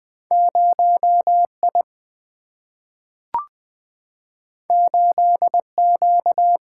Morse code, heard as a keyed tone, about 20 wpm, 700 Hz. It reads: 0 I 8 Q
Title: Single-Letter Single-Number 20wpm